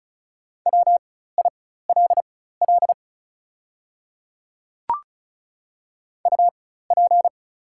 Morse code, heard as a keyed tone, about 36 wpm, spaced as for 16 wpm, 700 Hz. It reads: WILL UP
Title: 4x Character Spacing - Top 200 Words - 35wpm